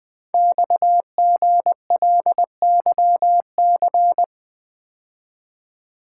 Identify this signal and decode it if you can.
Morse code, heard as a keyed tone, about 20 wpm, 700 Hz. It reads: XGLYC
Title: Code Groups - 5 Letters - 20wpm